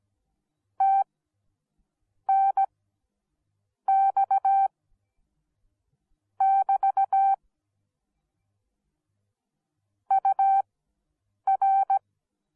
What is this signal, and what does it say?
Morse code, heard as a keyed tone, about 17 wpm, spaced as for 6 wpm, 780 Hz. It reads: TNX<BT> UR